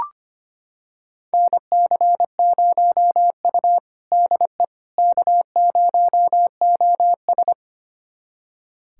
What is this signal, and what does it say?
Morse code, heard as a keyed tone, about 25 wpm, 700 Hz. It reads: NC0U DE K0OS
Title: USA Callsign DE Callsign - 4 Characters - 25wpm